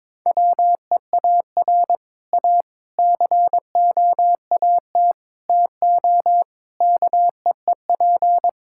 Morse code, heard as a keyed tone, about 22 wpm, 700 Hz. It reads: WEAR A COAT TO KEEP
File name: Difficult Sentences with Hints - 22wpm